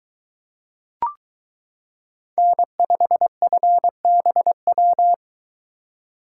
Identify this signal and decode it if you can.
Morse code, heard as a keyed tone, about 22 wpm, 700 Hz. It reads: N5FBW